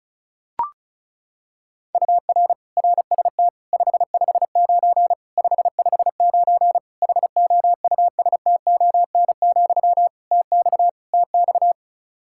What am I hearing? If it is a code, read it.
Morse code, heard as a keyed tone, about 34 wpm, 700 Hz. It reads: UR RST 559 559 HOUSTON, TX TX
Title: K1USN SST QSO Practice 35wpm